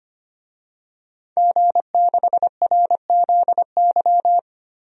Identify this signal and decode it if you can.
Morse code, heard as a keyed tone, about 25 wpm, 700 Hz. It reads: G6RZY
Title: DXCC-100 Call Signs - 5 Characters - 25wpm